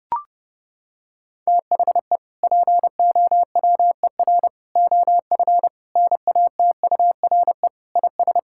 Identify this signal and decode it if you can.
Morse code, heard as a keyed tone, about 32 wpm, 700 Hz. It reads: THE POWER OF NATURE IS
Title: Sentences from Top 1000 Words - Review Part 5 - 30wpm